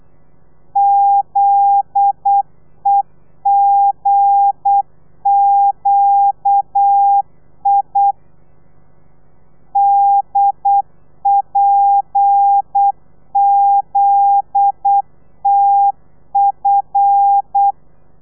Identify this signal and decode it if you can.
Morse code, heard as a keyed tone, about 8 wpm, 800 Hz. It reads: ZEGQI DPZTF
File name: telegrafia / i2tfj / Lez9 35